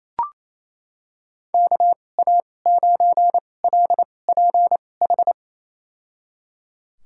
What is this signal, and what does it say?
Morse code, heard as a keyed tone, about 28 wpm, spaced as for 20 wpm, 700 Hz. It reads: KA9LPH
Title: USA Call Signs - 6 Characters - 2x Character Spacing - 28wpm